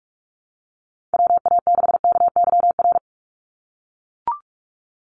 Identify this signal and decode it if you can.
Morse code, heard as a keyed tone, about 44 wpm, 700 Hz. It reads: WA6KYR
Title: NA SOTA Chaser Callsigns 45wpm